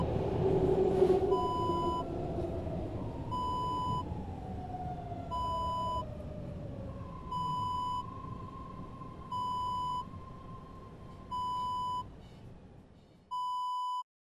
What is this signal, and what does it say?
There is no code.